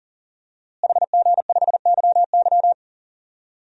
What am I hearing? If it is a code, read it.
Morse code, heard as a keyed tone, about 40 wpm, 700 Hz. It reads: HG5YY
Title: DXCC-100 Call Signs - 5 Characters - Repeated 3x - 40wpm